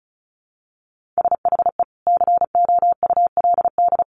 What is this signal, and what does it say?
Morse code, heard as a keyed tone, about 34 wpm, 700 Hz. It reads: SHE COULD
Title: Sentences from Top 1100 Words 35wpm